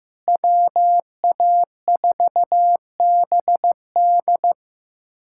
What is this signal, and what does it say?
Morse code, heard as a keyed tone, about 15 wpm, 700 Hz. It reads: WA4BD
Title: USA Callsign DE Callsign - 5 Characters - 15wpm